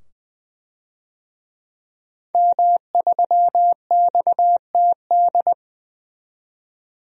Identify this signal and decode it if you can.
Morse code, heard as a keyed tone, about 21 wpm, 700 Hz. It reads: M3XTD
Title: DXCC-100 Call Signs - 5 Characters - 20wpm